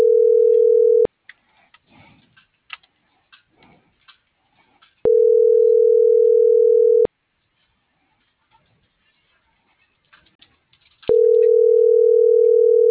Ringback tone (North America), 440 + 480 Hz.